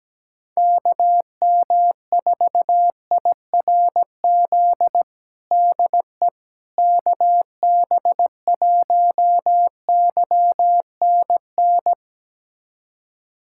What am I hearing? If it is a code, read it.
Morse code, heard as a keyed tone, about 17 wpm, 700 Hz. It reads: KM4IRZ DE KB1YNN
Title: USA Callsign DE Callsign - 6 Characters - 17wpm